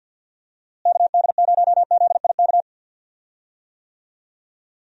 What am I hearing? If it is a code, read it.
Morse code, heard as a keyed tone, about 52 wpm, 700 Hz. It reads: KD0ZIK
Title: USA Call Signs - 6 Characters - Repeated 3x - 50wpm